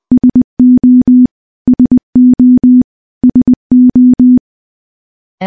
Morse code, heard as a keyed tone, about 20 wpm, 250 Hz. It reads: SO SO SO
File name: morse-training-20WPM